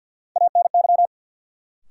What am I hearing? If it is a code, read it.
Morse code, heard as a keyed tone, about 52 wpm, 700 Hz. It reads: ANY
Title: Sets of 3 Words - New Word Drill - 50wpm